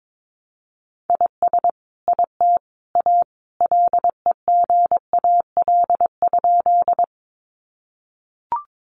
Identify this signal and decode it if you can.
Morse code, heard as a keyed tone, about 22 wpm, 700 Hz. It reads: IS IT A LEGAL?
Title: Sentences from Top 1200 Words 22wpm